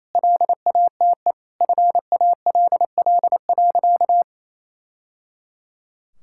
Morse code, heard as a keyed tone, about 28 wpm, 700 Hz. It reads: LATE FALL.
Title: Sentences from Top 1000 Words - Review Part 6 - 28wpm